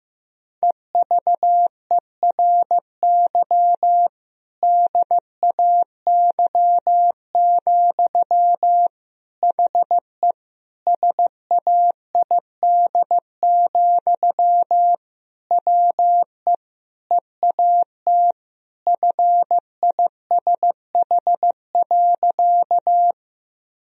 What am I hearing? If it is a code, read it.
Morse code, heard as a keyed tone, about 16 wpm, 700 Hz. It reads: EVERY DAY, HE SAID, WE EAT FISH.
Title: Sentences from Top 1000 Words - Review Part 1 - 15wpm